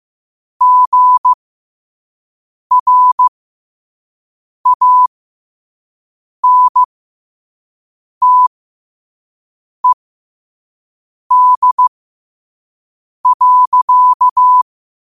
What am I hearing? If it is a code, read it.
Morse code, heard as a keyed tone, about 15 wpm, spaced as for 5 wpm, 1000 Hz. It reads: GRANTED.
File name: qotdfiles / 5wpm